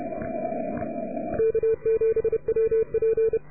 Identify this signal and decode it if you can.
Morse code, heard as a keyed tone, about 30 wpm, 440 Hz. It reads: K7WP